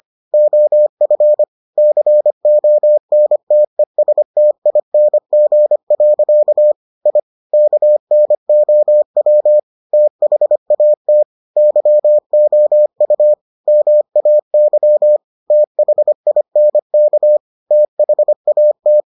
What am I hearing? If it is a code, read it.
Morse code, heard as a keyed tone, about 25 wpm, 600 Hz. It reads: OF CONTESTING. I KNOW THAT YOU MAY THINK THAT